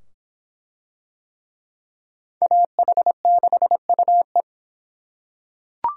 Morse code, heard as a keyed tone, about 25 wpm, 700 Hz. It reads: AH6UE